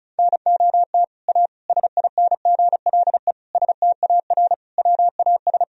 Morse code, heard as a keyed tone, about 34 wpm, 700 Hz. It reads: NOT A SINGLE STAR WAS
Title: Sentences from Top 1000 Words - Review Part 4 - 35wpm